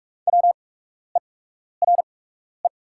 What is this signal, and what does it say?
Morse code, heard as a keyed tone, about 44 wpm, spaced as for 12 wpm, 690 Hz. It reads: WERE
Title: 8x Character Spacing - Top 200 Words - 45wpm